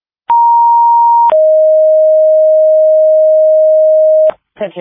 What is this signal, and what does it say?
Two-tone page, heard as a one-second tone, then a three-second tone, about 940 then 620 Hz, with speech. It is heard from 0.3 to 4.3 s.